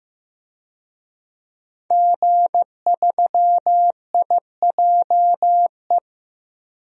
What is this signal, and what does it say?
Morse code, heard as a keyed tone, about 15 wpm, 700 Hz. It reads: G3IJE
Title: DXCC-100 Call Signs - 5 Characters - Repeated 2x - 15wpm